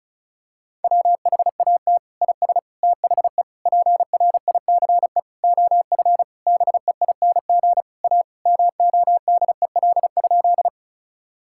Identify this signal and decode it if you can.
Morse code, heard as a keyed tone, about 36 wpm, 700 Hz. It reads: WHAT IS THE PRICE OF BEING A MODEL?
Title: Sentences from Top 1000 Words - Review Part 5 - 35wpm